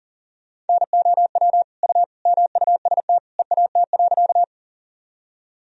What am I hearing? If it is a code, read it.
Morse code, heard as a keyed tone, about 40 wpm, 700 Hz. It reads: NOW U MUST EAT.
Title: Sentences from Top 300 Words and Abbreviations 40wpm